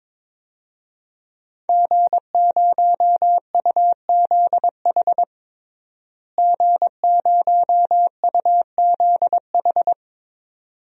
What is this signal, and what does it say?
Morse code, heard as a keyed tone, about 22 wpm, 700 Hz. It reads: G0UZH G0UZH
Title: DXCC-100 Call Signs - 5 Characters - Repeated 2x - 22wpm